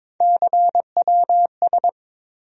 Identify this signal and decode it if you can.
Morse code, heard as a keyed tone, about 22 wpm, 700 Hz. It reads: CWS
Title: USA Call Sign Postfixes - 3 Letters 22wpm